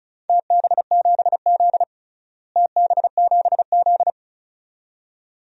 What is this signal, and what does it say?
Morse code, heard as a keyed tone, about 34 wpm, 700 Hz. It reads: TB7Z TB7Z
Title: DXCC-100 Call Signs - 4 Characters - Repeated 2x - 35wpm